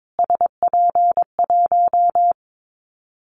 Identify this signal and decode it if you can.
Morse code, heard as a keyed tone, about 22 wpm, 700 Hz. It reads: SP1
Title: Callsign Prefixes for Top 25 DXCC - Repeated 3x - 22wpm